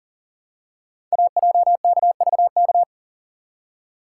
Morse code, heard as a keyed tone, about 39 wpm, 700 Hz. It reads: AJKVK
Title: Code Groups - 5 Letters - 40wpm